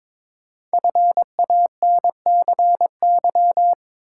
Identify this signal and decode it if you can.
Morse code, heard as a keyed tone, about 22 wpm, 700 Hz. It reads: FANCY
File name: 5 Letter Words - First 3 Letters Given - 22wpm